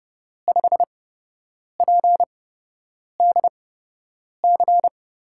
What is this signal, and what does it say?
Morse code, heard as a keyed tone, about 30 wpm, spaced as for 8 wpm, 700 Hz. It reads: 5PDC
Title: USA Call Signs - 6 Characters - 8x Character Spacing - 30wpm